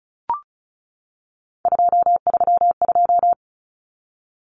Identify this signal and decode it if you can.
Morse code, heard as a keyed tone, about 34 wpm, 700 Hz. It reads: 232
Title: Common Cut Numbers - 3-Digit 35wpm